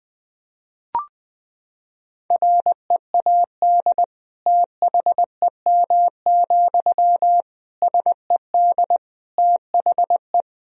Morse code, heard as a keyed tone, about 21 wpm, 700 Hz. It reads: READ THEM, SED THE
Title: Sentences from Top 400 Words and Abbreviations 20wpm